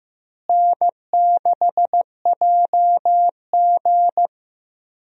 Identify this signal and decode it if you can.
Morse code, heard as a keyed tone, about 15 wpm, 700 Hz. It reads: N6JG
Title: USA Call Signs - 4 Characters - Repeated 2x - 15wpm